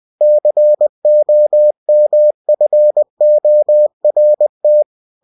Morse code, heard as a keyed tone, about 20 wpm, 600 Hz. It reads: COMFORT
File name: cw-books / winnie-the-pooh / winnie-20wpm-page-0101